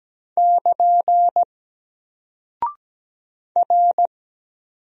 Morse code, heard as a keyed tone, about 17 wpm, 700 Hz. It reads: <KN> R
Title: Single Character (Letter:Number:Punctuation:Prosign) Alt 17wpm